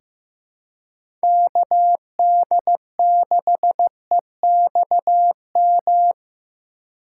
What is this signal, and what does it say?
Morse code, heard as a keyed tone, about 15 wpm, 700 Hz. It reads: KD6EXM
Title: USA Call Signs - 6 Characters - 2x Character Spacing - 15wpm